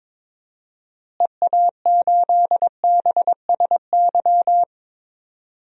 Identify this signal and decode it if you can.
Morse code, heard as a keyed tone, about 21 wpm, 700 Hz. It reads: EA8BSY
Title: DXCC-100 Call Signs - 6 Characters - Repeated 2x - 22wpm